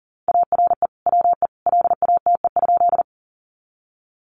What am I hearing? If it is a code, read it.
Morse code, heard as a keyed tone, about 37 wpm, 700 Hz. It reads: ARE WE LATE?
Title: Sentences from Top 300 Words 40wpm